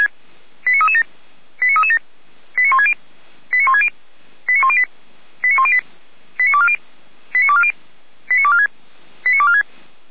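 Five-tone selective call (ZVEI), heard as quick runs of five tones, tones of about 70 ms. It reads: 89337 89337 89160 89160 89108 89108 89240 89240 89246 89246
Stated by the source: ZVEI five-tone call 89075 received by SDR.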